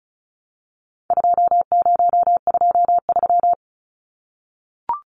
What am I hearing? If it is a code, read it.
Morse code, heard as a keyed tone, about 34 wpm, 700 Hz. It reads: 2023
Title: Numbers - 4-Digit 35wpm